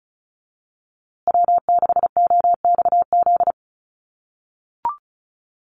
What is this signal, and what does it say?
Morse code, heard as a keyed tone, about 34 wpm, 700 Hz. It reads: W6OXZ
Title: USA Call Signs Repeated 2x 35wpm